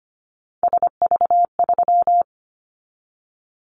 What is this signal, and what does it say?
Morse code, heard as a keyed tone, about 25 wpm, 700 Hz. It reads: SV3
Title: Callsign Prefixes for Top 50 DXCC - 25wpm